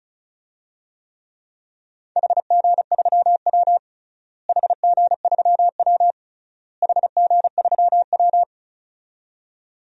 Morse code, heard as a keyed tone, about 34 wpm, 700 Hz. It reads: HG3W HG3W HG3W
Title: DXCC-100 Call Signs - 4 Characters - Repeated 3x - 35wpm